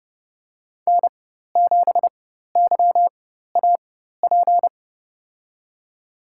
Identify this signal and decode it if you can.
Morse code, heard as a keyed tone, about 30 wpm, spaced as for 14 wpm, 700 Hz. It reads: N7YAP
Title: USA Call Signs - 5 Characters - 4x Character Spacing - 30wpm